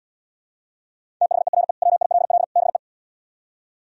Morse code, heard as a keyed tone, about 77 wpm, 700 Hz. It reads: THERE WILL BE